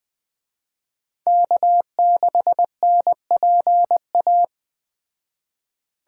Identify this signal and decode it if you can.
Morse code, heard as a keyed tone, about 19 wpm, 700 Hz. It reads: K6NPA